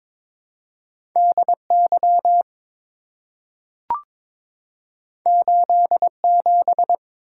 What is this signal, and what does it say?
Morse code, heard as a keyed tone, about 22 wpm, 700 Hz. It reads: DY 87
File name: practice / 22wpm / Two Letter-Number Combinations - 22wpm